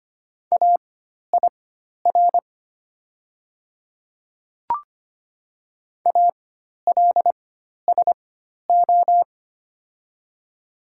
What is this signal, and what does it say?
Morse code, heard as a keyed tone, about 25 wpm, spaced as for 12 wpm, 700 Hz. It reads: AIR ALSO
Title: 4x Character Spacing - Top 200 Words - 25wpm